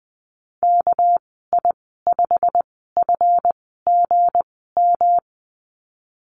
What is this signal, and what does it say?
Morse code, heard as a keyed tone, about 20 wpm, spaced as for 15 wpm, 700 Hz. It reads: KI5FGM
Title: 207 Lesson 33g - Letter j - Call Signs 20wpm